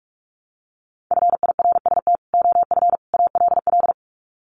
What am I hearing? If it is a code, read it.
Morse code, heard as a keyed tone, about 44 wpm, 700 Hz. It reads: FIRST OF ALL
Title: Sets of 3 Words 45wpm